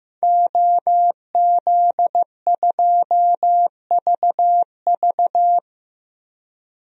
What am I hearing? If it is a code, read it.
Morse code, heard as a keyed tone, about 15 wpm, 700 Hz. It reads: OZ2VV
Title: DXCC-100 Call Signs - 5 Characters - Repeated 2x - 15wpm